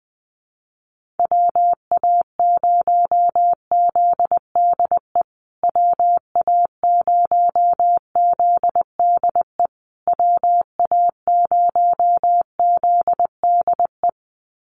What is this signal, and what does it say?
Morse code, heard as a keyed tone, about 19 wpm, 700 Hz. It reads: WA0ZDE WA0ZDE WA0ZDE
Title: USA Call Signs Repeated 3x 20wpm